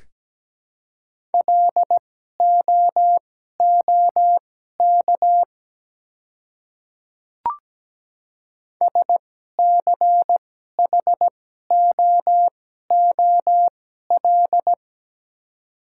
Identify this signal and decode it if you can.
Morse code, heard as a keyed tone, about 17 wpm, spaced as for 12 wpm, 700 Hz. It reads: LOOK SCHOOL
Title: 2x Character Spacing - Top 300 Words - 17wpm